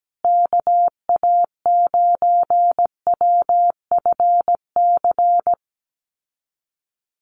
Morse code, heard as a keyed tone, about 17 wpm, 700 Hz. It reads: KA9WFC
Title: USA Call Signs Repeated 2x 17wpm